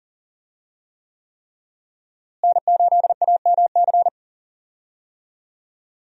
Morse code, heard as a keyed tone, about 40 wpm, 700 Hz. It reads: N8AMC
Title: USA Call Signs - 5 Characters - Repeated 3x - 40wpm